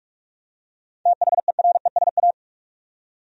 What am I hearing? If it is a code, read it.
Morse code, heard as a keyed tone, about 46 wpm, 700 Hz. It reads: THERESA
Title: USA Common Names - 45wpm